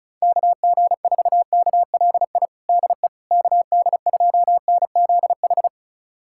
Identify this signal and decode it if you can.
Morse code, heard as a keyed tone, about 34 wpm, 700 Hz. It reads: KG4KLI DE KD2NZH
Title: USA Callsign DE Callsign 35wpm